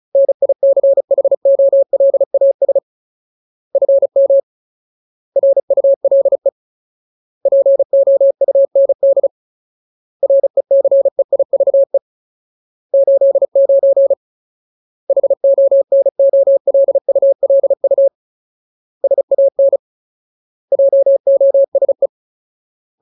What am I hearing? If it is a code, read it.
Morse code, heard as a keyed tone, about 36 wpm, 550 Hz. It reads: NICHOLAS FM RULE POUND RECEIVE 89 HONOLULU SAN JOSE
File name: Ultimate Word List / 35WPM Ultimate